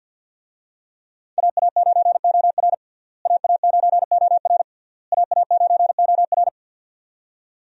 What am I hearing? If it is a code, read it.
Morse code, heard as a keyed tone, about 52 wpm, 700 Hz. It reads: AA9OR AA9OR AA9OR
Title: USA Call Signs - 5 Characters - Repeated 3x - 50wpm